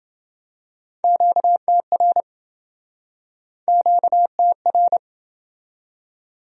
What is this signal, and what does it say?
Morse code, heard as a keyed tone, about 28 wpm, 700 Hz. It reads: QTR QTR